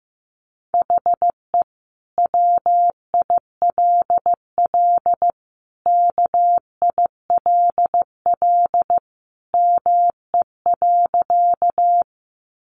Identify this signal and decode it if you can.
Morse code, heard as a keyed tone, about 15 wpm, 700 Hz. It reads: HE WILL KILL ME.
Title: Sentences from Top 1400 Words 15wpm